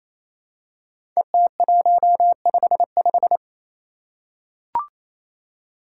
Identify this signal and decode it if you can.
Morse code, heard as a keyed tone, about 28 wpm, 700 Hz. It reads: ET155
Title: Five Challenging Letter-Number Combinations 28wpm